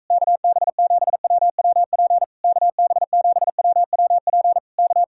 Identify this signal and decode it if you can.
Morse code, heard as a keyed tone, about 44 wpm, 700 Hz. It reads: KB7WWP KB7WWP K